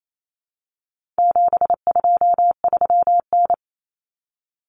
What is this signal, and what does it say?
Morse code, heard as a keyed tone, about 28 wpm, 700 Hz. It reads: 723N
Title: Common Cut Numbers - 4-Digit 28wpm